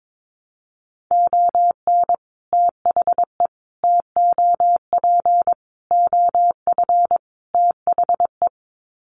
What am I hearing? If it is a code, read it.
Morse code, heard as a keyed tone, about 22 wpm, 700 Hz. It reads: ON THE TOP OF THE
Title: Sets of 5 Words Encore 22wpm